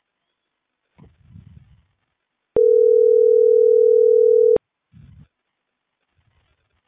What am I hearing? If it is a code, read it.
Ringback tone (North America), 440 + 480 Hz.